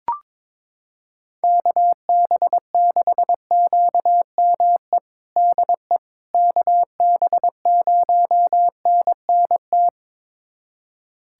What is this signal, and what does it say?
Morse code, heard as a keyed tone, about 22 wpm, 700 Hz. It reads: KB6QME DE KB0NNT
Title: USA Callsign DE Callsign - 6 Characters - 22wpm